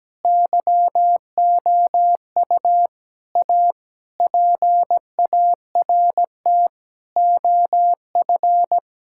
Morse code, heard as a keyed tone, about 17 wpm, 700 Hz. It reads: YOU A PART OF
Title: Sentences from Top 1000 Words - Review Part 6 - 17wpm